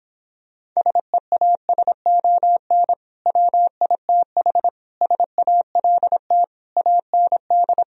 Morse code, heard as a keyed tone, about 26 wpm, 700 Hz. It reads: SEASON WITH SALT AND